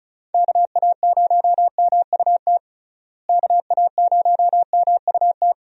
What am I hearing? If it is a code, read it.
Morse code, heard as a keyed tone, about 34 wpm, 700 Hz. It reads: KA0MUT KA0MUT
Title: USA Call Signs - 6 Characters - Repeated 3x - 35wpm